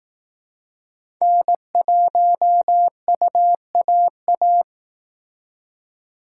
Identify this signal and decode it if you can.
Morse code, heard as a keyed tone, about 19 wpm, 700 Hz. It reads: N1UAA